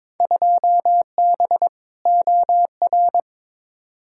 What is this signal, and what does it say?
Morse code, heard as a keyed tone, about 22 wpm, 700 Hz. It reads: 2B OR
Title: Field Day Exchanges Encore - 22wpm